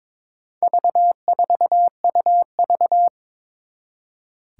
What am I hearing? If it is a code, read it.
Morse code, heard as a keyed tone, about 22 wpm, 700 Hz. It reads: V4UV
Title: Four Challenging Letter-Number Combinations 22wpm